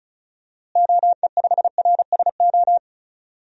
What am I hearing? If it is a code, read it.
Morse code, heard as a keyed tone, about 36 wpm, 700 Hz. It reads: OE5RSO
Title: DXCC-100 Call Signs - 6 Characters - Repeated 2x - 35wpm